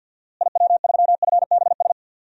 Morse code, heard as a keyed tone, about 49 wpm, 700 Hz. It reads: IW3FBS
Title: DXCC-100 Call Signs - 6 Characters - Repeated 2x - 50wpm